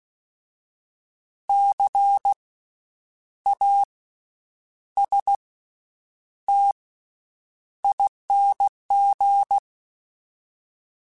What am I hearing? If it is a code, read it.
Morse code, heard as a keyed tone, about 16 wpm, spaced as for 8 wpm, 800 Hz. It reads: CASTING
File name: Special INGs / Special INGs-mp3s / VerySlowToFaster Ing @15